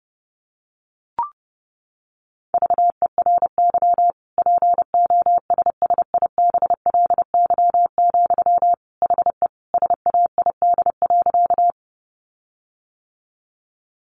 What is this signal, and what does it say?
Morse code, heard as a keyed tone, about 30 wpm, 700 Hz. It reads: VERY POSSIBLY, HE SAID.
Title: Sentences from Top 1300 Words 30wpm